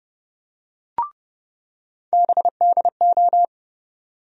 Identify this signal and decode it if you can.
Morse code, heard as a keyed tone, about 30 wpm, 700 Hz. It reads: BDO
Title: USA Call Sign Postfixes - 3 Letters 30wpm